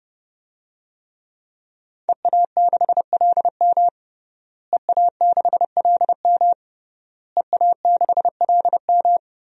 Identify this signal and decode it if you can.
Morse code, heard as a keyed tone, about 30 wpm, 700 Hz. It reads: EA6LM EA6LM EA6LM
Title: DXCC-100 Call Signs - 5 Characters - Repeated 3x - 30wpm